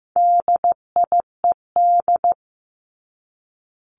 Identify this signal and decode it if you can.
Morse code, heard as a keyed tone, about 15 wpm, 700 Hz. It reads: DIED
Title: ED Suffix - Top 100 Words with Base Words 15wpm